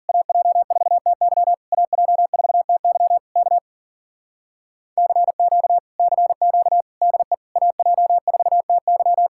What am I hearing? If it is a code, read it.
Morse code, heard as a keyed tone, about 50 wpm, 700 Hz. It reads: AJ4TY AJ4TY K CQ CQ DE AJ4TY
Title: Speed-Racing Common QSO Phrases Encore - 40wpm